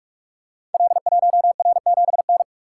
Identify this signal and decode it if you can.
Morse code, heard as a keyed tone, about 44 wpm, 700 Hz. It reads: L1R7N
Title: Code Groups - 5 Letter:Numbers - 45wpm